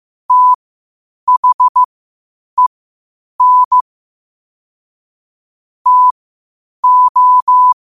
Morse code, heard as a keyed tone, about 15 wpm, spaced as for 8 wpm, 1000 Hz. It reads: THEN TO